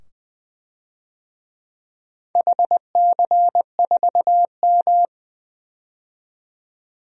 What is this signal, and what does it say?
Morse code, heard as a keyed tone, about 20 wpm, 700 Hz. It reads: HC4M
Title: DXCC-100 Call Signs - 4 Characters - 20wpm